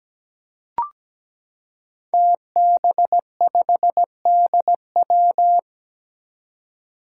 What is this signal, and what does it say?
Morse code, heard as a keyed tone, about 17 wpm, 700 Hz. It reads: TB5DW
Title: Five Challenging Letter-Number Combinations 17wpm